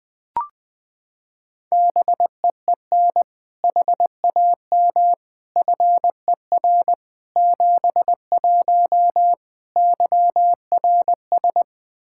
Morse code, heard as a keyed tone, about 21 wpm, 700 Hz. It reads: BEEN HAM FER 71 YRS
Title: Practice QSO - Exchange 3 - More Info - 20wpm